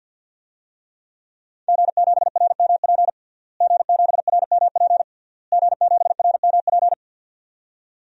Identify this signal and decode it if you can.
Morse code, heard as a keyed tone, about 51 wpm, 700 Hz. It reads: G7RMP G7RMP G7RMP
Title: DXCC-100 Call Signs - 5 Characters - Repeated 3x - 50wpm